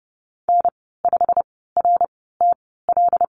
Morse code, heard as a keyed tone, about 30 wpm, spaced as for 17 wpm, 700 Hz. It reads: N5RTL
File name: Lesson 15g - Number 5 - Call Signs 30wpm